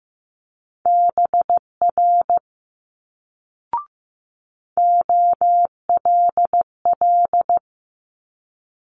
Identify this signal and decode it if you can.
Morse code, heard as a keyed tone, about 15 wpm, 700 Hz. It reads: BR OLL
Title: Multi-letter Phonemes Quick Fire 15wpm